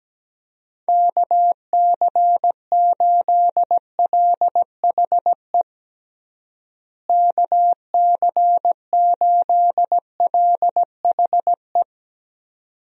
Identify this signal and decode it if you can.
Morse code, heard as a keyed tone, about 17 wpm, 700 Hz. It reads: KC8LHE KC8LHE
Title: USA Call Signs - 6 Characters - Repeated 2x - 17wpm